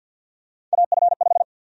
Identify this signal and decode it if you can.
Morse code, heard as a keyed tone, about 47 wpm, 700 Hz. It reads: AF5
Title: USA Call Sign Prefixes - Repeated 2x - 50wpm